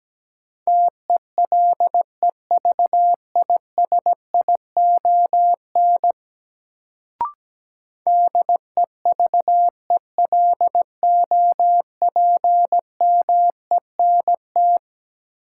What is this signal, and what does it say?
Morse code, heard as a keyed tone, about 17 wpm, 700 Hz. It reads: TELEVISION DEVELOPMENT
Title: Sentences from Top 1000 Words - Review Part 4 - New Word Drill - 17wpm